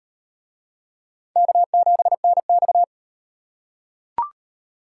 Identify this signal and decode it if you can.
Morse code, heard as a keyed tone, about 37 wpm, 700 Hz. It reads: K7NX